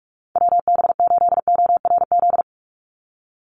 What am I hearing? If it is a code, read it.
Morse code, heard as a keyed tone, about 44 wpm, 700 Hz. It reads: WB8ORZ
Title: USA Call Signs 45wpm